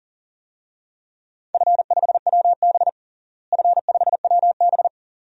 Morse code, heard as a keyed tone, about 37 wpm, 700 Hz. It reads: F5WB F5WB
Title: DXCC-100 Call Signs - 4 Characters - Repeated 2x - 40wpm